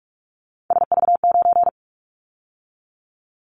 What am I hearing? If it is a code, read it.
Morse code, heard as a keyed tone, about 46 wpm, 700 Hz. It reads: SV9
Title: Callsign Prefixes for Top 100 DXCC - 45wpm